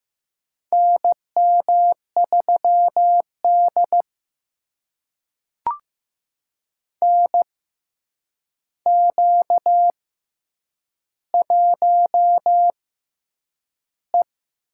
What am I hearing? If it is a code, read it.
Morse code, heard as a keyed tone, about 15 wpm, spaced as for 8 wpm, 700 Hz. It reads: NM3D NQ1E
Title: USA Call Signs - 4 Characters - 6x Character Spacing - 15wpm